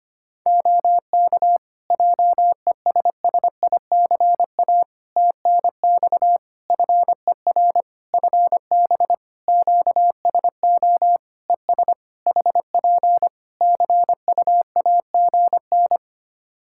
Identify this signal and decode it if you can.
Morse code, heard as a keyed tone, about 25 wpm, 700 Hz. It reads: OK JESSICA TNX FER FB QSO ES HP CUAGN
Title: Practice QSO - Exchange 4 - Ending - 25wpm